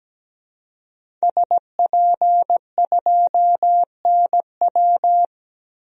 Morse code, heard as a keyed tone, about 17 wpm, 700 Hz. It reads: SP2NW
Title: DXCC-100 Call Signs - 5 Characters - Repeated 2x - 17wpm